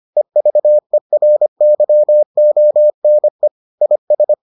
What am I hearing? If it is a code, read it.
Morse code, heard as a keyed tone, about 25 wpm, 600 Hz. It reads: EVERYONE IS